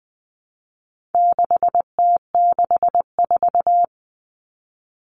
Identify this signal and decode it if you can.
Morse code, heard as a keyed tone, about 20 wpm, 700 Hz. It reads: 6T64
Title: All Cut Numbers - 4-Digit 20wpm